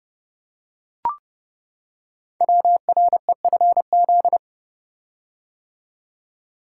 Morse code, heard as a keyed tone, about 30 wpm, 700 Hz. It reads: WREFZ
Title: Code Groups - 5 Letter:Numbers - 30wpm